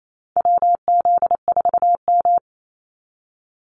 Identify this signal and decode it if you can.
Morse code, heard as a keyed tone, about 28 wpm, 700 Hz. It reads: WZ4M
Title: NA SOTA Activator Callsigns 28wpm